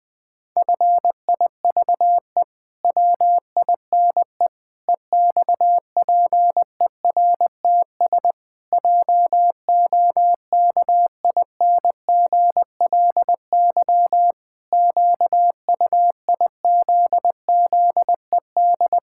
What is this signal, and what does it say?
Morse code, heard as a keyed tone, about 20 wpm, 700 Hz. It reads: FIVE WINE EXPERTS JOKINGLY QUIZZED
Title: Pangram Challenge v2 - 20wpm